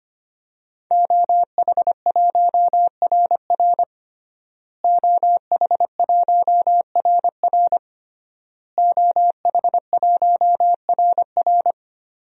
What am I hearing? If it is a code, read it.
Morse code, heard as a keyed tone, about 25 wpm, 700 Hz. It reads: OH1RR OH1RR OH1RR
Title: DXCC-100 Call Signs - 5 Characters - Repeated 3x - 25wpm